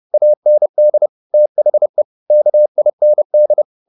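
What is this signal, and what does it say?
Morse code, heard as a keyed tone, about 30 wpm, 600 Hz. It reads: AND THE KIND
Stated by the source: Morse code at 30 wpm, the novel Never by Ken Follett in English, from CW med Gnister.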